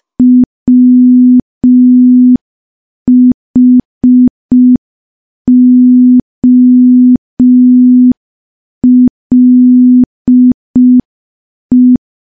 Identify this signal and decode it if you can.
Morse code, heard as a keyed tone, about 5 wpm, 250 Hz. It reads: WHOLE
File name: morse / morse-training-05WPM